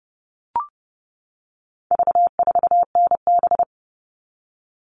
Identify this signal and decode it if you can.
Morse code, heard as a keyed tone, about 30 wpm, 700 Hz. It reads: V4NB
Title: All Cut Numbers - 4-Digit 30wpm